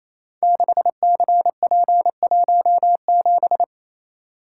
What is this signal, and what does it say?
Morse code, heard as a keyed tone, about 28 wpm, 700 Hz. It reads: 6CP17
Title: Code Groups - 5 Letter:Numbers - 28wpm